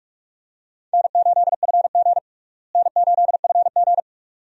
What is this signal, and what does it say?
Morse code, heard as a keyed tone, about 45 wpm, 700 Hz. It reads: N8FG N8FG
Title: USA Call Signs - 4 Characters - Repeated 2x - 45wpm